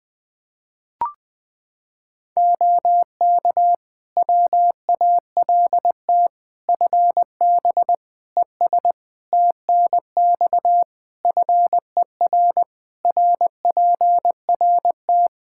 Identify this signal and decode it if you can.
Morse code, heard as a keyed tone, about 20 wpm, 700 Hz. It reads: OK WALT FB ES TNX FER RPRT
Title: Practice QSO - Exchange 2 - Info - 20wpm